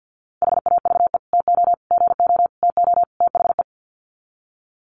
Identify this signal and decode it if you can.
Morse code, heard as a keyed tone, about 50 wpm, 700 Hz. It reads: HAVE TO GO TO THE
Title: Sets of 5 Words Encore 50wpm